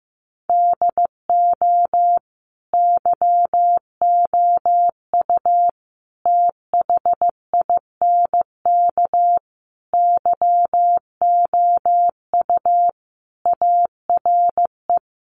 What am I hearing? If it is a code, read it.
Morse code, heard as a keyed tone, about 15 wpm, 700 Hz. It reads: DO YOU THINK YOU ARE
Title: Sentences from Top 1200 Words 15wpm